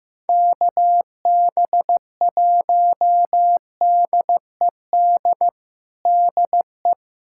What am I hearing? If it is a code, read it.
Morse code, heard as a keyed tone, about 15 wpm, 700 Hz. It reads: KB1DED DE
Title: USA Callsign DE Callsign - 6 Characters - 15wpm